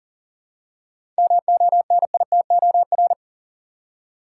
Morse code, heard as a keyed tone, about 40 wpm, 700 Hz. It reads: MONITOR
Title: Seven Letter Words - 40wpm